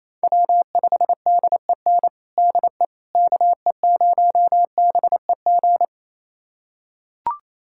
Morse code, heard as a keyed tone, about 28 wpm, 700 Hz. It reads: W5DEN DE KE0BEG
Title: USA Callsign DE Callsign 28wpm